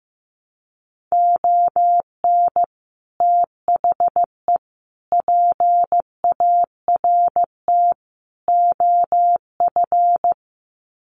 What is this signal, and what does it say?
Morse code, heard as a keyed tone, about 15 wpm, 700 Hz. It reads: ON THE PART OF
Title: Sets of 4 Words 15wpm